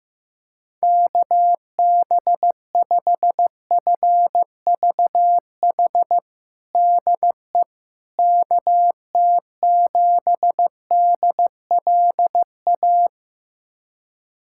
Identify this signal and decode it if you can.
Morse code, heard as a keyed tone, about 15 wpm, 700 Hz. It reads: KB5FVH DE KT7DLA
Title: USA Callsign DE Callsign 15wpm